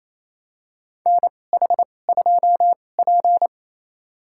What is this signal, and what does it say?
Morse code, heard as a keyed tone, about 28 wpm, spaced as for 20 wpm, 700 Hz. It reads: NH2P
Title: USA Call Signs - 4 Characters - 2x Character Spacing - 28wpm